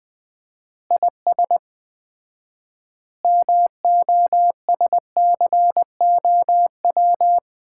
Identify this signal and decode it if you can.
Morse code, heard as a keyed tone, about 21 wpm, 700 Hz. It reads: IS MOSCOW
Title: Common QSO Phrases - 4x Word-Spacing - 20wpm